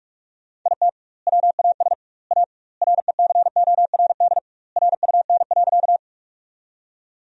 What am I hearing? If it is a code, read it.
Morse code, heard as a keyed tone, about 46 wpm, 700 Hz. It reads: IT WAS A RECORD RUN.
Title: 2x Word-Spacing - Sentences from Top 500 Words - 45wpm